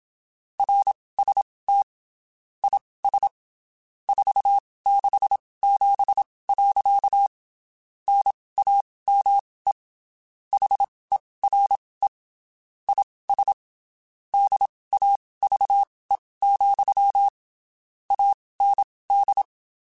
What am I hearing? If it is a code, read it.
Morse code, heard as a keyed tone, about 27 wpm, spaced as for 19 wpm, 800 Hz. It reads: RST IS 467. NAME HERE IS DAVE, AND
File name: (spaced) / QSOs Spaced-mp3s / kc7sac @25 sp